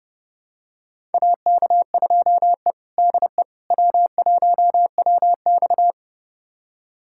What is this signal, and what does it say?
Morse code, heard as a keyed tone, about 30 wpm, 700 Hz. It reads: AK2E DE W1WX
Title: USA Callsign DE Callsign - 4 Characters - 30wpm